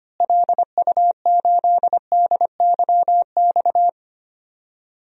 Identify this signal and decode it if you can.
Morse code, heard as a keyed tone, about 25 wpm, 700 Hz. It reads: LU8DYX
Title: DXCC-100 Call Signs - 6 Characters - Repeated 3x - 25wpm